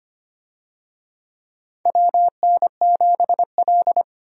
Morse code, heard as a keyed tone, about 25 wpm, 700 Hz. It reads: WN7L
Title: USA Call Signs - 4 Characters - 25wpm